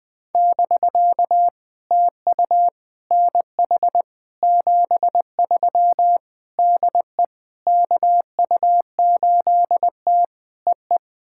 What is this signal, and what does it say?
Morse code, heard as a keyed tone, about 20 wpm, 700 Hz. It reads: <BK> TU NH 73 DE KU8T EE
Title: POTA QSO Practice - 20wpm